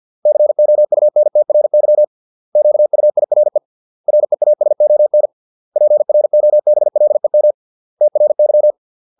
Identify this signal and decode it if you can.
Morse code, heard as a keyed tone, about 50 wpm, 600 Hz. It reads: COUNTRY QUIRE REASON PROBLEM TRY